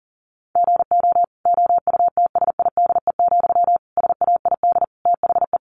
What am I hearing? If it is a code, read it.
Morse code, heard as a keyed tone, about 40 wpm, 700 Hz. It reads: GO OUTSIDE, SAID THE
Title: Sentences from Top 900 Words 40wpm